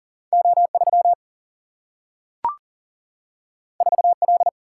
Morse code, heard as a keyed tone, about 40 wpm, 700 Hz. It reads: O 3 4 L
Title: Single-Letter Single-Number 40wpm